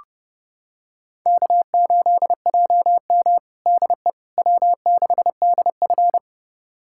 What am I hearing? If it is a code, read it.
Morse code, heard as a keyed tone, about 30 wpm, 700 Hz. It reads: K8JM DE W6DF
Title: USA Callsign DE Callsign - 4 Characters - 30wpm